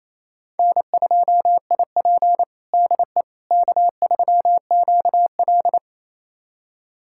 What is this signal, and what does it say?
Morse code, heard as a keyed tone, about 28 wpm, 700 Hz. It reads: N2IP DE K3QL
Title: USA Callsign DE Callsign - 4 Characters - 28wpm